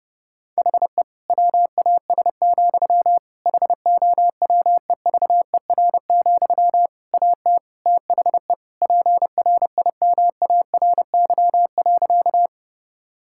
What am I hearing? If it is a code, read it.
Morse code, heard as a keyed tone, about 32 wpm, 700 Hz. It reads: HE WAS, HOWEVER, AT THE PRIMARY.
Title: Sentences from Top 1000 Words - Review Part 5 - 30wpm